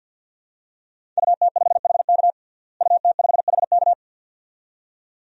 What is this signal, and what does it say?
Morse code, heard as a keyed tone, about 52 wpm, 700 Hz. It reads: UT5HK UT5HK
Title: DXCC-100 Call Signs - 5 Characters - Repeated 2x - 50wpm